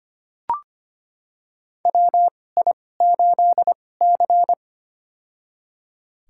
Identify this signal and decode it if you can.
Morse code, heard as a keyed tone, about 25 wpm, spaced as for 18 wpm, 700 Hz. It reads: WI8C
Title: USA Call Signs - 4 Characters - 2x Character Spacing - 25wpm